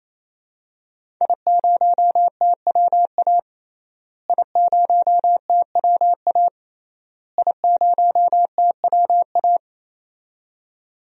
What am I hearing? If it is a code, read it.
Morse code, heard as a keyed tone, about 28 wpm, 700 Hz. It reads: I0TWA I0TWA I0TWA
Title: DXCC-100 Call Signs - 5 Characters - Repeated 3x - 28wpm